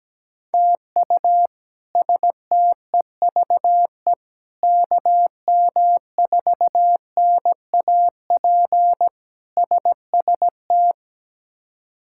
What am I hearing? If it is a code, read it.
Morse code, heard as a keyed tone, about 17 wpm, 700 Hz. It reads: TU STEVE KM4NAP SST
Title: K1USN SST QSO Practice - 17wpm